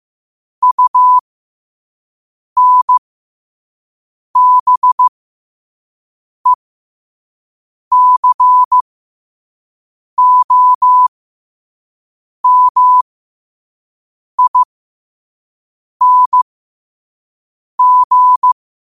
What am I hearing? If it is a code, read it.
Morse code, heard as a keyed tone, about 15 wpm, spaced as for 5 wpm, 1000 Hz. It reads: UNBECOMING
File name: qotdfiles / 5wpm